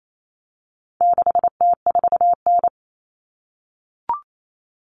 Morse code, heard as a keyed tone, about 28 wpm, 700 Hz. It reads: 6T4N
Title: Common Cut Numbers - 4-Digit 28wpm